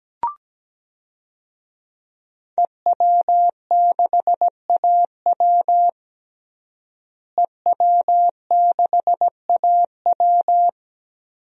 Morse code, heard as a keyed tone, about 17 wpm, 700 Hz. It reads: EW6AW EW6AW
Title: DXCC-100 Call Signs - 5 Characters - Repeated 2x - 17wpm